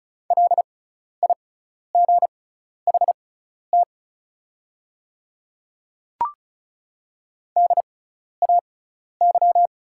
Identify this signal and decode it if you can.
Morse code, heard as a keyed tone, about 34 wpm, spaced as for 12 wpm, 700 Hz. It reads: LIGHT DAY
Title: 6x Character Spacing - Top 200 Words - 35wpm